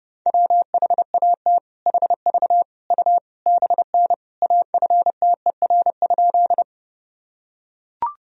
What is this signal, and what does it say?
Morse code, heard as a keyed tone, about 30 wpm, 700 Hz. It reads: WHAT HV U BN AFTER?
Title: Sentences from Top 200 Words and Abbreviations 30wpm